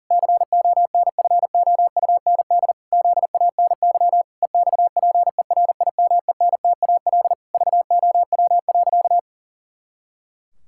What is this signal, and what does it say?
Morse code, heard as a keyed tone, about 40 wpm, 700 Hz. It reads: CONFOUND ZANY EXPERIMENTAL VOW.
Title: Pangram Challenge v2 - 40wpm